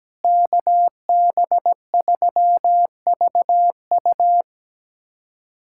Morse code, heard as a keyed tone, about 17 wpm, 700 Hz. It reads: KB3VU
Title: USA Callsign DE Callsign - 5 Characters - 17wpm